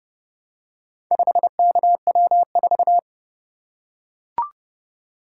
Morse code, heard as a keyed tone, about 28 wpm, 700 Hz. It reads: 5KW4